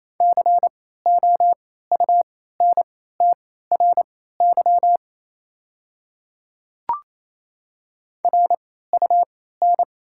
Morse code, heard as a keyed tone, about 27 wpm, spaced as for 16 wpm, 700 Hz. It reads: COUNTRY RUN
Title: 3x Character Spacing - Top 300 Words - 28wpm